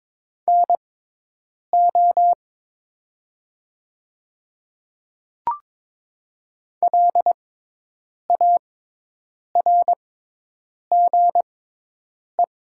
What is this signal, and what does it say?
Morse code, heard as a keyed tone, about 22 wpm, spaced as for 8 wpm, 700 Hz. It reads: NO LARGE
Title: 6x Character Spacing - Top 200 Words - 22wpm